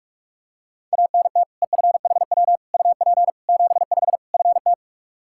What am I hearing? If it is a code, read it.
Morse code, heard as a keyed tone, about 46 wpm, 700 Hz. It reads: ANT EFHW UP 75 FT
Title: Practice QSO - Exchange 2 - Info - 45wpm